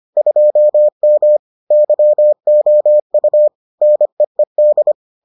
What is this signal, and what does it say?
Morse code, heard as a keyed tone, about 25 wpm, 600 Hz. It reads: 2M YOU NEED